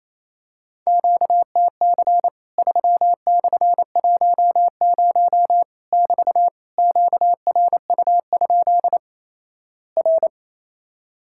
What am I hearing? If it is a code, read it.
Morse code, heard as a keyed tone, about 28 wpm, 700 Hz. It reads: QTC 3/10 <BT> QRU? R
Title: Worked All Europe DX Contest - QTC Exchange - 28wpm